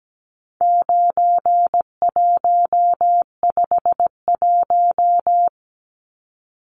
Morse code, heard as a keyed tone, about 17 wpm, 700 Hz. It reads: 9151
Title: Numbers - 4-Digit 17wpm